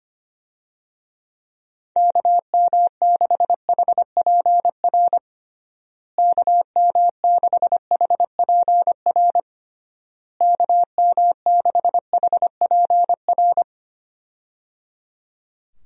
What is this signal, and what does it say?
Morse code, heard as a keyed tone, about 25 wpm, 700 Hz. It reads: KM6HPR KM6HPR KM6HPR
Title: USA Call Signs - 6 Characters - Repeated 3x - 25wpm